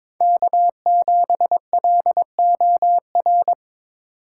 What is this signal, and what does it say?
Morse code, heard as a keyed tone, about 22 wpm, 700 Hz. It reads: K7LOR
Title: USA Call Signs - 5 Characters - 6x Character Spacing - 22wpm